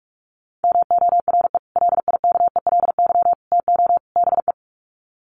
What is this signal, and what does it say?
Morse code, heard as a keyed tone, about 44 wpm, 700 Hz. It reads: MORE LIKELY TO BE
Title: Sets of 4 Words 45wpm